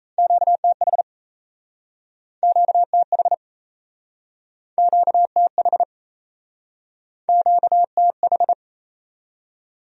Morse code, heard as a keyed tone, about 39 wpm, 700 Hz. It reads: QTH QTH QTH QTH